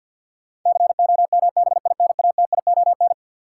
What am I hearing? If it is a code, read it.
Morse code, heard as a keyed tone, about 53 wpm, 700 Hz. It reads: COMBINATION
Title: Sentences from Top 1000 Words - Review Part 6 - New Word Drill - 50wpm